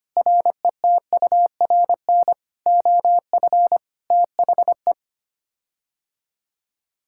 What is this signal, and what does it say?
Morse code, heard as a keyed tone, about 25 wpm, 700 Hz. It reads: RETURN OF THE
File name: Sets of 9 Words - 25wpm